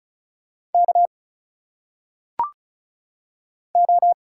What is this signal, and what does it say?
Morse code, heard as a keyed tone, about 36 wpm, 700 Hz. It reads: K O
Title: Single Letters - Phonetic Pronunciation 35wpm